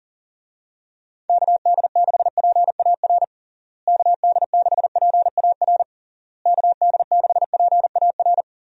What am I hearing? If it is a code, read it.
Morse code, heard as a keyed tone, about 40 wpm, 700 Hz. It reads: KD6PAR KD6PAR KD6PAR
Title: USA Call Signs - 6 Characters - Repeated 3x - 40wpm